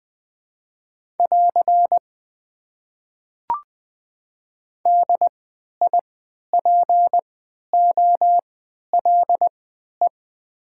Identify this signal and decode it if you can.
Morse code, heard as a keyed tone, about 20 wpm, spaced as for 11 wpm, 700 Hz. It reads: <AR> DIPOLE
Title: Top 90 QSO Elements - 3x Character Spacing - 20wpm